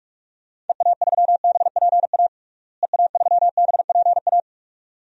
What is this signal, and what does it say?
Morse code, heard as a keyed tone, about 41 wpm, 700 Hz. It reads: EA3BPA EA3BPA
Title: DXCC-100 Call Signs - 6 Characters - Repeated 2x - 45wpm